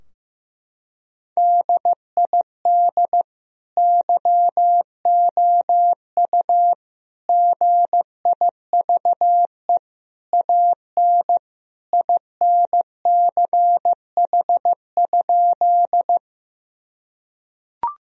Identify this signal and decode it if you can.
Morse code, heard as a keyed tone, about 14 wpm, 700 Hz. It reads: DID YOU GIVE AN INCH?
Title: Sentences from Top 1000 Words - Review Part 3 - 15wpm